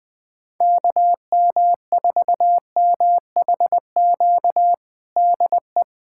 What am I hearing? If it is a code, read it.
Morse code, heard as a keyed tone, about 20 wpm, 700 Hz. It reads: KM4MHQ DE
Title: USA Callsign DE Callsign - 6 Characters - 20wpm